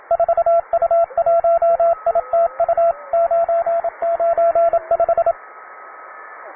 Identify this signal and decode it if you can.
Morse code, heard as a keyed tone, about 27 wpm, 680 Hz, with radio noise. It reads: 4U1ITU 995
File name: wae / cw beginner / py2rh cw